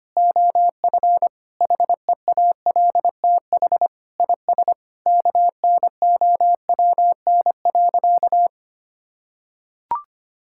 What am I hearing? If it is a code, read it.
Morse code, heard as a keyed tone, about 25 wpm, 700 Hz. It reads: OF HEALTH IS KNOWN.
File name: practice / 25wpm / Sentences with 3-Word Phrases - 25wpm